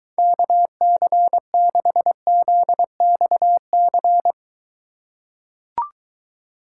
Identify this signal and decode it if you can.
Morse code, heard as a keyed tone, about 23 wpm, 700 Hz. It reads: KC6ZXC